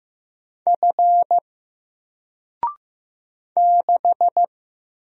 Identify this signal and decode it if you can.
Morse code, heard as a keyed tone, about 15 wpm, 700 Hz. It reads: F 6
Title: Single Letter-Number Alt 15wpm